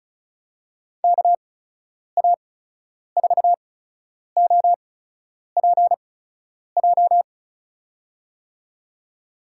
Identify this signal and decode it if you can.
Morse code, heard as a keyed tone, about 36 wpm, spaced as for 10 wpm, 700 Hz. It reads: KA4OPJ
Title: USA Call Signs - 8x Character Spacing - 35wpm